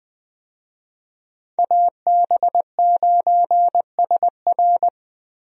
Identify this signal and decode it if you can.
Morse code, heard as a keyed tone, about 20 wpm, 700 Hz. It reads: AB9SR
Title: USA Call Signs - 5 Characters - Repeated 2x - 20wpm